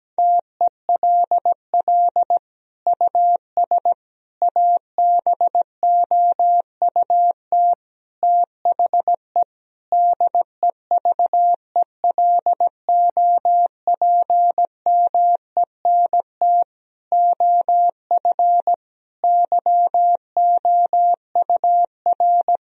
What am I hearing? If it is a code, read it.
Morse code, heard as a keyed tone, about 17 wpm, 700 Hz. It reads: TELL US ABOUT THE DEVELOPMENT OF YOUR
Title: Sentences from Top 1000 Words - Review Part 4 - 17wpm